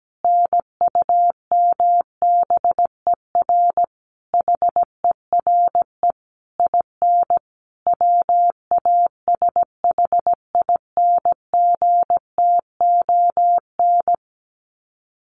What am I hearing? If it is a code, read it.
Morse code, heard as a keyed tone, about 17 wpm, 700 Hz. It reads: NUMBER HERE IN WASHINGTON
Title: Sets of 5 Words Encore2 17wpm